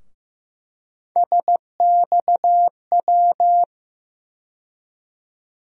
Morse code, heard as a keyed tone, about 15 wpm, 700 Hz. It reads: SXW
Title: USA Call Sign Postfixes - 3 Letters 15wpm